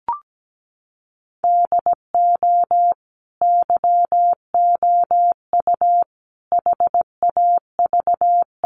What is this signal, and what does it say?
Morse code, heard as a keyed tone, about 17 wpm, 700 Hz. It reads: DO YOU HAV
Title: Sentences from Top 1500 Words 17wpm